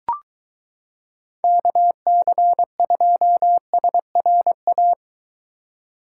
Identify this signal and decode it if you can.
Morse code, heard as a keyed tone, about 22 wpm, 700 Hz. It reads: KC2SRA